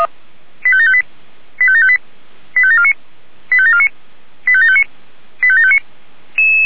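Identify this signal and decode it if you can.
Five-tone selective call (ZVEI), heard as quick runs of five tones, tones of about 70 ms. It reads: 85758 85758 85749 85749 85759 85759
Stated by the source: ZVEI five-tone selective call received on an SDR.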